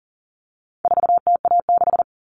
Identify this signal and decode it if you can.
Morse code, heard as a keyed tone, about 40 wpm, 700 Hz. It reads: 4TA6
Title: All Cut Numbers - 4-Digit 40wpm